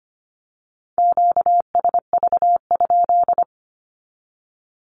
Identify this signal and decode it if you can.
Morse code, heard as a keyed tone, about 25 wpm, 700 Hz. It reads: QSV?
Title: Q-Codes with Spoken Meaning - 25wpm